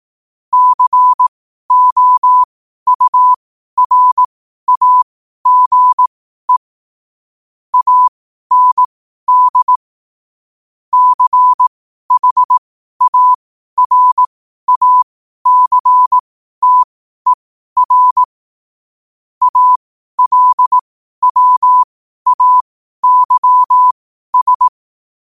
Morse code, heard as a keyed tone, about 18 wpm, spaced as for 12 wpm, 1000 Hz. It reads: COURAGE AND CHARACTER ALWAYS
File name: qotdfiles / 12wpm